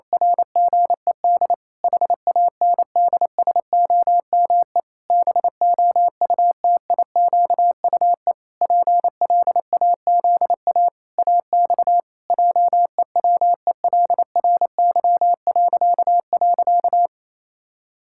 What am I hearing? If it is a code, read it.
Morse code, heard as a keyed tone, about 28 wpm, 700 Hz. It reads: RGED HANDSOME BOUTIQUE PLAZA AX JEWELRY..
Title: Pangram Challenge v2 - 28wpm